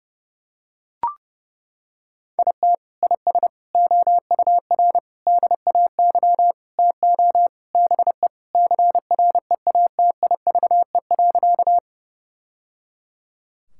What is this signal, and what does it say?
Morse code, heard as a keyed tone, about 30 wpm, 700 Hz. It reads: IT IS OUR DAY TO BE CREATIVE.
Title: Sentences from Top 1000 Words - Review Part 5 - 30wpm